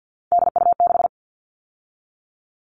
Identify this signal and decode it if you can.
Morse code, heard as a keyed tone, about 50 wpm, 700 Hz. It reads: DU6
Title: All Cut Numbers - 3-Digit 50wpm